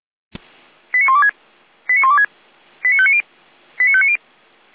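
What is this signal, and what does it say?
Five-tone selective call (ZVEI), heard as quick runs of five tones, tones of about 80 ms. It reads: 89126 89126 89590 89590
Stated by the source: Five-tone ZVEI call received on an SDR.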